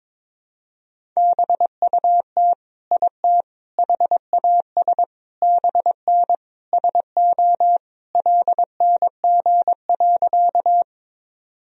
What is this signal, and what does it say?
Morse code, heard as a keyed tone, about 22 wpm, 700 Hz. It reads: BUT IT HAS BN SO LNG.
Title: Sentences from Top 100 Words and Abbreviations 22wpm